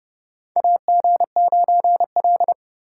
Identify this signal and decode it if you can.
Morse code, heard as a keyed tone, about 32 wpm, 700 Hz. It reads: AG9L
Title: USA Call Signs - 4 Characters - 2x Character Spacing - 30wpm